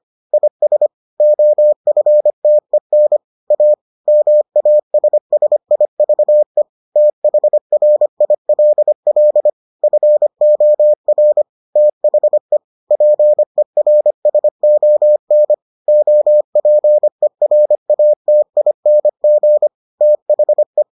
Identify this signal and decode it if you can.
Morse code, heard as a keyed tone, about 25 wpm, 600 Hz. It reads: IS OFTEN A MASSIVE THRILL FOR THE PERSON OPERATING THE